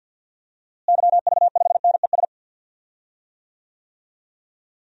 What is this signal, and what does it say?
Morse code, heard as a keyed tone, about 48 wpm, 700 Hz. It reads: YV5NES
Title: DXCC-100 Call Signs - 6 Characters - Repeated 2x - 50wpm